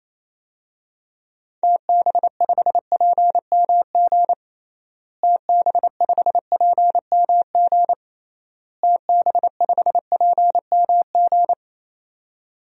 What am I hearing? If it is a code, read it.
Morse code, heard as a keyed tone, about 28 wpm, 700 Hz. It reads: TB5PMG TB5PMG TB5PMG
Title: DXCC-100 Call Signs - 6 Characters - Repeated 3x - 28wpm